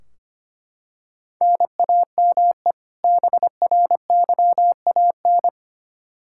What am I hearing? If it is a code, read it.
Morse code, heard as a keyed tone, about 25 wpm, 700 Hz. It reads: NAME BRYAN
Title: K1USN SST QSO Practice 25wpm